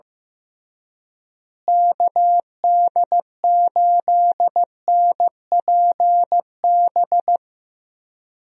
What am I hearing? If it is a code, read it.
Morse code, heard as a keyed tone, about 15 wpm, 700 Hz. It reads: KD8NPB
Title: USA Call Signs - 6 Characters - Repeated 3x - 15wpm